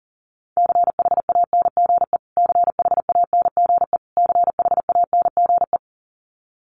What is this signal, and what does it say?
Morse code, heard as a keyed tone, about 40 wpm, 700 Hz. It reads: CHANGE CHANGE CHANGE
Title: Top 200 Words Repeated 40wpm